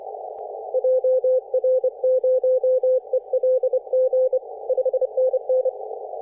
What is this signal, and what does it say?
Morse code, heard as a keyed tone, about 26 wpm, 510 Hz. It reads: JR0ELG 5NN
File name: 3D2YA 24.913MHz CW